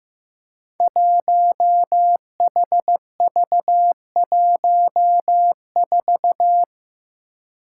Morse code, heard as a keyed tone, about 15 wpm, 700 Hz. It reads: 1HV14
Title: Five Challenging Letter-Number Combinations 15wpm